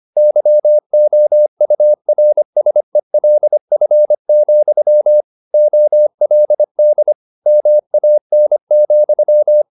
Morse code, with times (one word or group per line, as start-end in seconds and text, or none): 0.2-5.2 YOURSELF,
5.5-7.1 OLD
7.5-9.6 MAN,